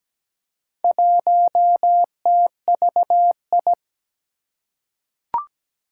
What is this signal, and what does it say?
Morse code, heard as a keyed tone, about 17 wpm, 700 Hz. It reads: 1TVI
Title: Four Challenging Letter-Number Combinations 17wpm